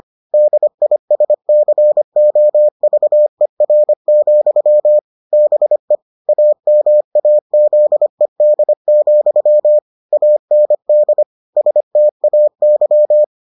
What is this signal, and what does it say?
Morse code, heard as a keyed tone, about 25 wpm, 600 Hz. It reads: DISCOVER, BE AMAZED, AND STAY